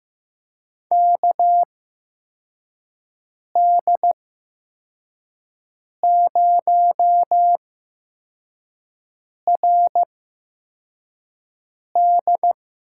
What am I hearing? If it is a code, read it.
Morse code, heard as a keyed tone, about 15 wpm, spaced as for 4 wpm, 700 Hz. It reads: KD0RD
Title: USA Call Signs - 6 Characters - 8x Character Spacing - 15wpm